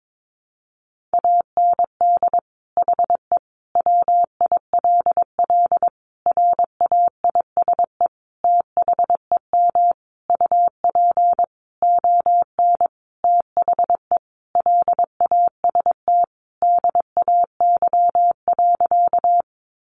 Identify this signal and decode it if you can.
Morse code, heard as a keyed tone, about 22 wpm, 700 Hz. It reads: AND HE WILL RAISE THEM UP ON THE LAST DAY.
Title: Sentences from Top 1400 Words 22wpm